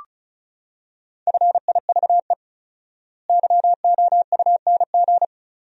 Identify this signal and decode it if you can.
Morse code, heard as a keyed tone, about 36 wpm, 700 Hz. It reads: FIVE YOUNG
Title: 4x Word-Spacing - Sentences from Top 400 Words - 35wpm